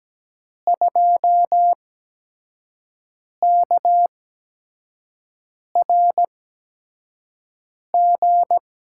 Morse code, heard as a keyed tone, about 17 wpm, spaced as for 5 wpm, 700 Hz. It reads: 2KRG
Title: USA Call Signs - 8x Character Spacing - 17wpm